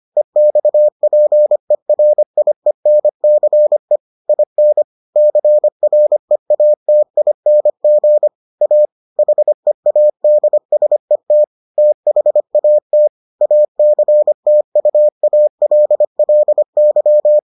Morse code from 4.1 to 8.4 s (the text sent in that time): IN CREATING